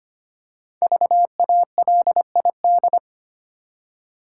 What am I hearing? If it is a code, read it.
Morse code, heard as a keyed tone, about 25 wpm, 700 Hz. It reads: VALID
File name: 5 Letter Words - First 3 Letters Given - 25wpm